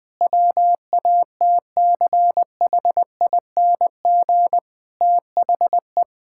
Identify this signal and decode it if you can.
Morse code, heard as a keyed tone, about 20 wpm, 700 Hz. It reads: WATCHING THE
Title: Difficult Sentences with Hints - 20wpm